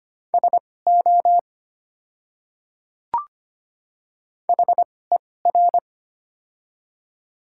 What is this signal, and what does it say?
Morse code, heard as a keyed tone, about 25 wpm, spaced as for 18 wpm, 700 Hz. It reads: SO HER
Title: 2x Character Spacing - Top 100 Words - 25wpm